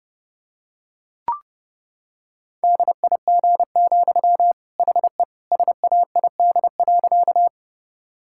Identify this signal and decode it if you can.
Morse code, heard as a keyed tone, about 28 wpm, 700 Hz. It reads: DIG, HE SAID.
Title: Sentences from Top 1600 Words 30wpm